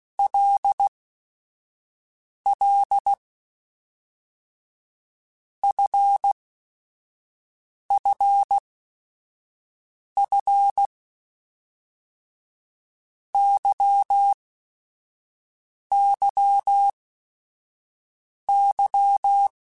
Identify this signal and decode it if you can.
Morse code, heard as a keyed tone, about 16 wpm, 800 Hz. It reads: L L F F F Y Y Y
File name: Basic Sets / Basic Letter Groups Collective7 / LFCYQ@5